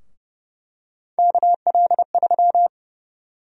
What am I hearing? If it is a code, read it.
Morse code, heard as a keyed tone, about 30 wpm, 700 Hz. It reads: KL3
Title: USA Call Sign Prefixes - Repeated 2x - 30wpm